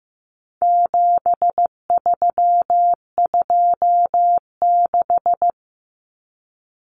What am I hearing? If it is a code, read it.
Morse code, heard as a keyed tone, about 15 wpm, 700 Hz. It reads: 7326
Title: Common Cut Numbers - 4-Digit 15wpm